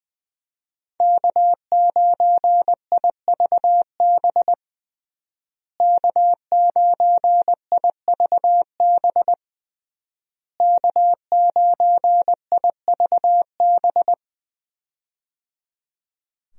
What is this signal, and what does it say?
Morse code, heard as a keyed tone, about 20 wpm, 700 Hz. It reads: K9IVB K9IVB K9IVB
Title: USA Call Signs - 5 Characters - Repeated 3x - 20wpm